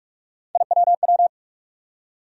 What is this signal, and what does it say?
Morse code, heard as a keyed tone, about 46 wpm, 700 Hz. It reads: IWW
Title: USA Call Sign Postfixes - 3 Letters 45wpm